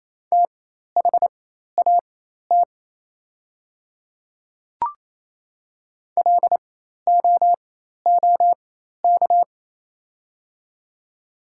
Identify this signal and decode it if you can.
Morse code, heard as a keyed tone, about 28 wpm, spaced as for 13 wpm, 700 Hz. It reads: THAT LOOK
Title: 4x Character Spacing - Top 100 Words - 28wpm